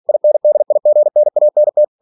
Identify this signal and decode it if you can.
Morse code, heard as a keyed tone, about 46 wpm, 600 Hz. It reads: INDIGNANT
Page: Never 47wpm